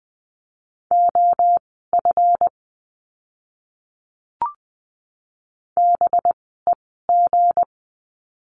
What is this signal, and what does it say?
Morse code, heard as a keyed tone, about 20 wpm, spaced as for 15 wpm, 700 Hz. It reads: OF BEG
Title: 185 Lesson 30f - Letter b - Words 20wpm